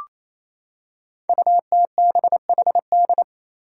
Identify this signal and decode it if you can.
Morse code, heard as a keyed tone, about 28 wpm, 700 Hz. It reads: UTBHD
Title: Five Challenging Letter-Number Combinations 28wpm